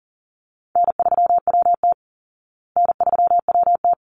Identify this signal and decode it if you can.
Morse code, heard as a keyed tone, about 40 wpm, 700 Hz. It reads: N3WT N3WT
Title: Easy Call Signs Repeated 2x - 40wpm